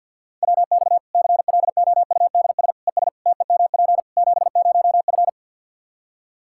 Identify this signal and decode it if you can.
Morse code, heard as a keyed tone, about 47 wpm, 700 Hz. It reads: WX CLOUDS ES TEMP 70F
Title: Practice QSO - Exchange 2 - Info - 50wpm